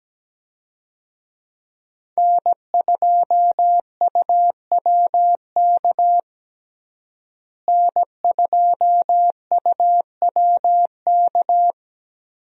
Morse code, heard as a keyed tone, about 17 wpm, 700 Hz. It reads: N2UWK N2UWK
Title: USA Call Signs - 5 Characters - Repeated 3x - 17wpm